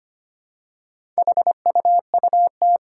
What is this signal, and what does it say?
Morse code, heard as a keyed tone, about 25 wpm, 700 Hz. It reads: HUUT
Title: Four Challenging Letter-Number Combinations 25wpm